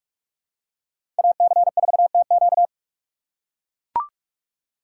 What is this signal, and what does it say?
Morse code, heard as a keyed tone, about 44 wpm, 700 Hz. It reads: AC4TQ
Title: USA Call Signs - 2x Character Spacing - 45wpm